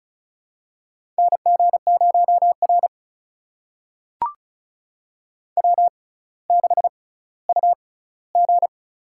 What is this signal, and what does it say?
Morse code, heard as a keyed tone, about 36 wpm, spaced as for 18 wpm, 700 Hz. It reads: NG0R W6UG
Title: USA Call Signs - 6x Character Spacing - 35wpm